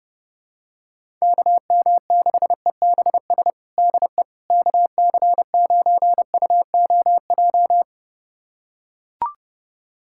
Morse code, heard as a keyed tone, about 32 wpm, 700 Hz. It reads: KM6EBS DE KC9UOJ
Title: USA Callsign DE Callsign - 6 Characters - 30wpm